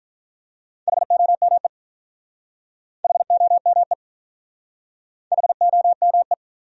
Morse code, heard as a keyed tone, about 46 wpm, 700 Hz. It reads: HOME HOME HOME